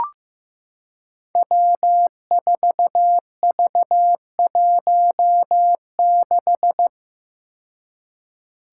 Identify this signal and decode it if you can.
Morse code, heard as a keyed tone, about 15 wpm, 700 Hz. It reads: W4V16
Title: Five Challenging Letter-Number Combinations 15wpm